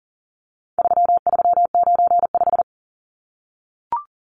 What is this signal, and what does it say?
Morse code, heard as a keyed tone, about 40 wpm, 700 Hz. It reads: 3395
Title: Numbers - 4-Digit 40wpm